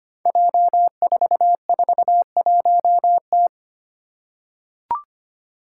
Morse code, heard as a keyed tone, about 25 wpm, 700 Hz. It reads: J441T
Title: Five Challenging Letter-Number Combinations 25wpm